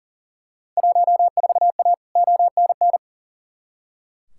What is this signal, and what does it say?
Morse code, heard as a keyed tone, about 40 wpm, 700 Hz. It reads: 14A ONN
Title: Field Day Exchanges Encore - 40wpm